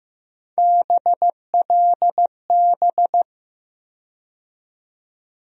Morse code, heard as a keyed tone, about 15 wpm, 700 Hz. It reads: BLB
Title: USA Call Sign Postfixes - 3 Letters 15wpm